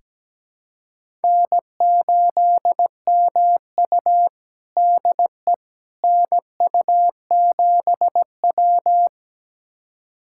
Morse code, heard as a keyed tone, about 17 wpm, 700 Hz. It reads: N8MU DE NU7W
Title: USA Callsign DE Callsign - 4 Characters - 17wpm